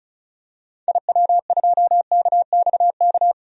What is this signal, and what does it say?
Morse code, heard as a keyed tone, about 34 wpm, 700 Hz. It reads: IW2KXK
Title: DXCC-100 Call Signs - 6 Characters - 35wpm